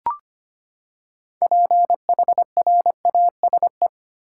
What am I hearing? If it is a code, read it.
Morse code, heard as a keyed tone, about 25 wpm, 700 Hz. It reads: PHRASE
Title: Six Letter Words - 25wpm